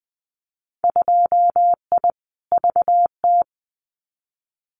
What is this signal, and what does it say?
Morse code, heard as a keyed tone, about 20 wpm, 700 Hz. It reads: 2I VT
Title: Winter Field Day Exchanges 20wpm